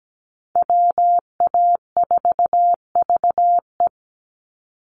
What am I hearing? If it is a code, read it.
Morse code, heard as a keyed tone, about 17 wpm, 700 Hz. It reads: WA4VE
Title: USA Call Signs 17wpm